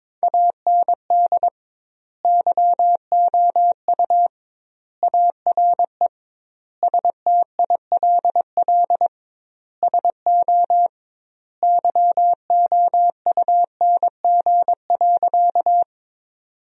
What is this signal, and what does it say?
Morse code, heard as a keyed tone, about 22 wpm, 700 Hz. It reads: AND YOU ARE STILL SO YOUNG.
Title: 2x Word-Spacing - Sentences from Top 400 Words - 22wpm